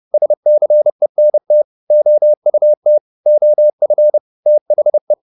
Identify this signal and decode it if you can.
Morse code, heard as a keyed tone, about 30 wpm, 600 Hz. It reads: SCENT OUT OF THE